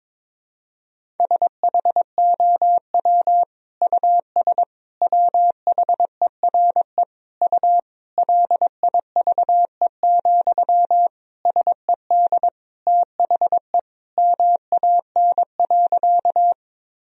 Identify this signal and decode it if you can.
Morse code, heard as a keyed tone, about 23 wpm, 700 Hz. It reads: SHOW US WHERE U LIVE, SED THE MAN.
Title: Sentences from Top 200 Words and Abbreviations 22wpm